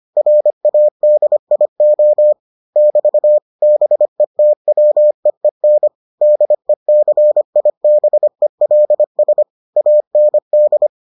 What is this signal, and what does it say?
Morse code, heard as a keyed tone, about 25 wpm, 600 Hz. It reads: RADIO <BT> BETWEEN DECIBELS AND